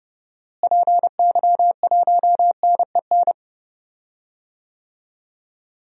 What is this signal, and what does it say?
Morse code, heard as a keyed tone, about 30 wpm, 700 Hz. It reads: PY1NEN
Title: DXCC-100 Call Signs - 6 Characters - Repeated 3x - 30wpm